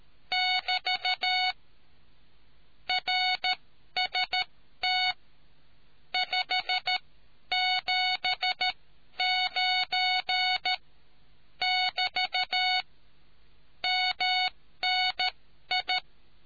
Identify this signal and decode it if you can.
Morse code, heard as a keyed tone, about 13 wpm, spaced as for 11 wpm, 740 Hz. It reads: <BT> RST 579 <BT> MNI